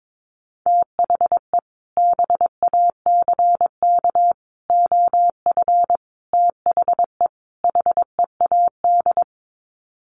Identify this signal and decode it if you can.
Morse code, heard as a keyed tone, about 22 wpm, 700 Hz. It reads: THE BACK OF THE HEAD
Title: Sets of 5 Words Encore2 22wpm